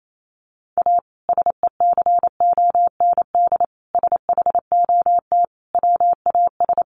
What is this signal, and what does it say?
Morse code, heard as a keyed tone, about 28 wpm, 700 Hz. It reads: A SECOND SHOT WAS
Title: Sentences from Top 1100 Words 28wpm